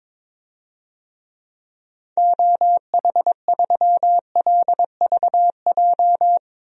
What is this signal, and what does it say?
Morse code, heard as a keyed tone, about 22 wpm, 700 Hz. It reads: OH3LVJ
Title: DXCC-100 Call Signs - 6 Characters - 22wpm